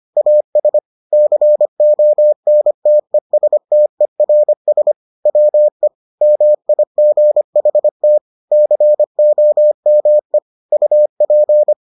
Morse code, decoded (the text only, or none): AS CONTESTERS WE MIGHT COME UP